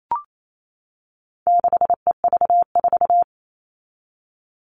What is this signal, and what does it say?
Morse code, heard as a keyed tone, about 28 wpm, 700 Hz. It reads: 6EV4
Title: All Cut Numbers - 4-Digit 28wpm